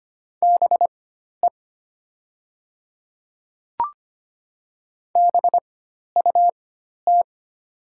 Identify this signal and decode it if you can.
Morse code, heard as a keyed tone, about 25 wpm, spaced as for 12 wpm, 700 Hz. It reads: BE BUT
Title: 4x Character Spacing - Top 200 Words - 25wpm